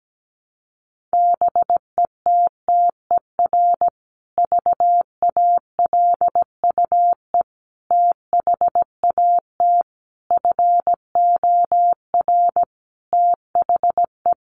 Morse code, heard as a keyed tone, about 17 wpm, 700 Hz. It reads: BETTER VALUE THAT FOR THE
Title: Sentences from Top 600 Words 17wpm